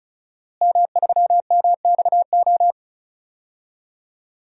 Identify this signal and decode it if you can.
Morse code, heard as a keyed tone, about 34 wpm, 700 Hz. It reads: M3MXO
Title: DXCC-100 Call Signs - 5 Characters - Repeated 2x - 35wpm